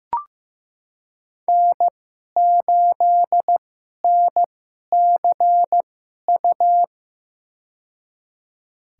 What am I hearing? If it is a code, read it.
Morse code, heard as a keyed tone, about 15 wpm, spaced as for 11 wpm, 700 Hz. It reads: N8NCU
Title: USA Call Signs - 5 Characters - 2x Character Spacing - 15wpm